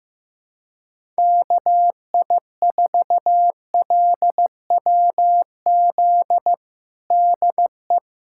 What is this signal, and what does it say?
Morse code, heard as a keyed tone, about 16 wpm, 700 Hz. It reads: KI4LWZ DE
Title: USA Callsign DE Callsign - 6 Characters - 15wpm